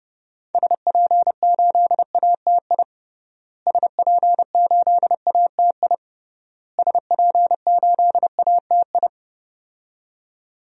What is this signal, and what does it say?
Morse code, heard as a keyed tone, about 29 wpm, 700 Hz. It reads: SP8ATI SP8ATI SP8ATI
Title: DXCC-100 Call Signs - 6 Characters - Repeated 3x - 30wpm